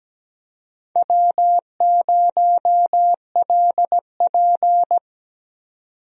Morse code, heard as a keyed tone, about 17 wpm, 700 Hz. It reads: W0LP
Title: USA Call Signs - 4 Characters - 17wpm